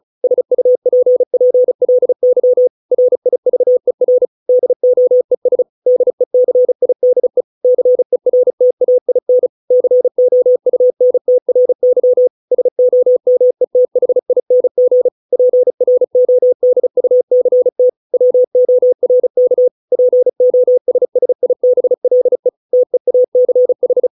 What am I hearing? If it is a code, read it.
Morse code, heard as a keyed tone, about 35 wpm, 500 Hz. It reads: SUPPLY RIVER DOES DECIDE CERTAIN COUNTRY SOMETHING PRODUCT WORK POSSIBLE TEACH